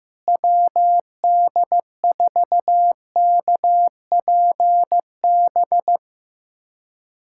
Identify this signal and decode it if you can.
Morse code, heard as a keyed tone, about 15 wpm, 700 Hz. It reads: WD4KPB
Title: USA Callsign DE Callsign 15wpm